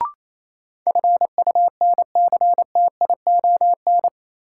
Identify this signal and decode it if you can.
Morse code, heard as a keyed tone, about 28 wpm, 700 Hz. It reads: FUNCTION
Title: Sentences from Top 1100 Words - New Word Drill - 28wpm